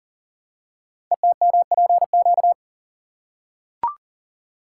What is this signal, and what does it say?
Morse code, heard as a keyed tone, about 39 wpm, 690 Hz. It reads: ETMPQ
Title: Code Groups - 5 Letters - 40wpm